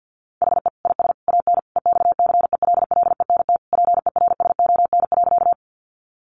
Morse code, heard as a keyed tone, about 50 wpm, 700 Hz. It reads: HE IS AN EXCELLENT PERSON.
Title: Sentences from Top 1000 Words 50wpm